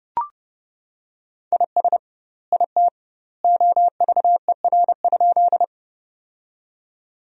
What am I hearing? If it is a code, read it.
Morse code, heard as a keyed tone, about 30 wpm, 700 Hz. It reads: IS IT OVER?
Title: 2x Word-Spacing - Sentences from Top 100 Words - 30wpm